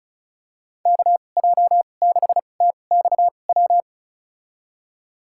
Morse code, heard as a keyed tone, about 36 wpm, spaced as for 25 wpm, 700 Hz. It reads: KJ6TXW
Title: USA Call Signs - 6 Characters - 2x Character Spacing - 35wpm